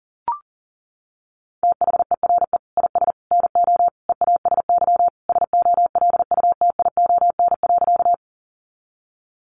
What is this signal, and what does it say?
Morse code, heard as a keyed tone, about 40 wpm, 700 Hz. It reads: THERE IS NO EASY SOLUTION.
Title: Sentences from Top 1300 Words 40wpm